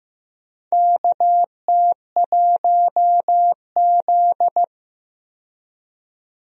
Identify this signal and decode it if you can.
Morse code, heard as a keyed tone, about 15 wpm, 700 Hz. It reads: KT1Z
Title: USA Call Signs - 4 Characters - 15wpm